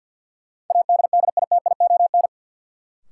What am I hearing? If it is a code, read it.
Morse code, heard as a keyed tone, about 52 wpm, 700 Hz. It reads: ADDITION
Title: Sentences from Top 1000 Words - Review Part 6 - New Word Drill - 50wpm